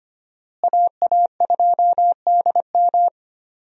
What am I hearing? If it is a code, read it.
Morse code, heard as a keyed tone, about 25 wpm, 700 Hz. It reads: AA2DM
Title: USA Call Signs - 5 Characters - Repeated 2x - 25wpm